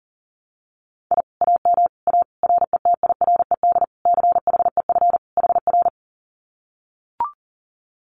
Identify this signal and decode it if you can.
Morse code, heard as a keyed tone, about 40 wpm, 700 Hz. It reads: I AM A RETIRED CHEF HR
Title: Common QSO Phrases Encore 40wpm